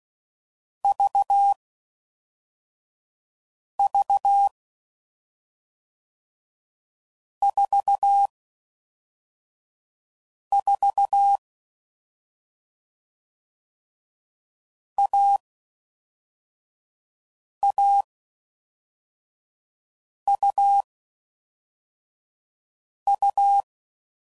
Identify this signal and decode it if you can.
Morse code, heard as a keyed tone, about 16 wpm, 800 Hz. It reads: V V 4 4 A A U U
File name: Combinations / Difficult Combos / AUV AUV4